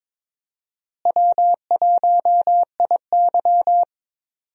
Morse code, heard as a keyed tone, about 21 wpm, 700 Hz. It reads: W1IY
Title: USA Call Signs - 4 Characters - Repeated 2x - 22wpm